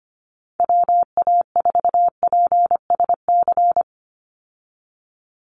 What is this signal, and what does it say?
Morse code, heard as a keyed tone, about 25 wpm, 700 Hz. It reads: WA4PSC
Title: Call Signs Repeated 2x - 25wpm